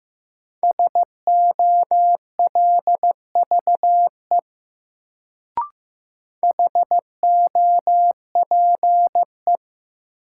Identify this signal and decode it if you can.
Morse code, heard as a keyed tone, about 15 wpm, 700 Hz. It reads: SOLVE HOPE
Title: Sentences from Top 1000 Words - New Word Drill - 15wpm